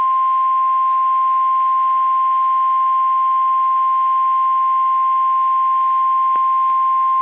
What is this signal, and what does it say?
Emergency Alert System (SAME) tones (the 1050 Hz attention tone). It is heard from 0.6 to 6.6 s.
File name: eas / KSDZKDJL / KSDZKDJL